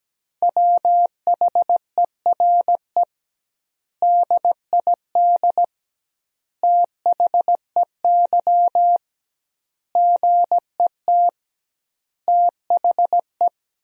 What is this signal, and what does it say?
Morse code, heard as a keyed tone, about 17 wpm, 700 Hz. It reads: WHERE DID THEY GET THE
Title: 2x Word-Spacing - Sentences from Top 400 Words - 17wpm